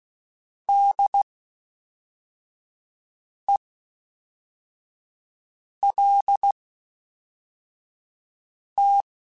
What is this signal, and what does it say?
Morse code, heard as a keyed tone, about 16 wpm, spaced as for 4 wpm, 800 Hz. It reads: DELT